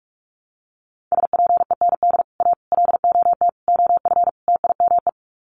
Morse code, heard as a keyed tone, about 44 wpm, 700 Hz. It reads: SPEND A LOT OF TIME
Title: Sets of 5 Words 45wpm